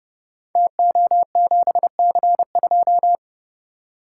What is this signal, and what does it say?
Morse code, heard as a keyed tone, about 30 wpm, 700 Hz. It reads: TO7C2
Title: Code Groups - 5 Letter:Numbers - 30wpm